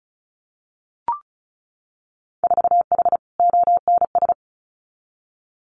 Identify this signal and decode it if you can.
Morse code, heard as a keyed tone, about 34 wpm, 700 Hz. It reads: 4H ONS
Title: Winter Field Day Exchanges Encore 35wpm